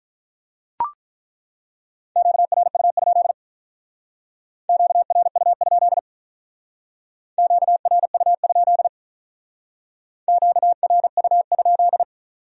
Morse code, heard as a keyed tone, about 42 wpm, 700 Hz. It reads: QRU? QRU? QRU? QRU?
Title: Speed-Racing Q-Codes - 35wpm